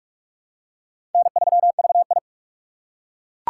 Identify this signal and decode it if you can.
Morse code, heard as a keyed tone, about 42 wpm, 690 Hz. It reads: N3VI
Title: USA Call Signs - 4 Characters - 3x Character Spacing - 45wpm